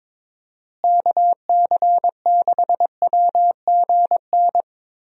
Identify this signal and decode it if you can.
Morse code, heard as a keyed tone, about 22 wpm, 700 Hz. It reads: KC6WGN
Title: USA Call Signs - 6 Characters - Repeated 2x - 22wpm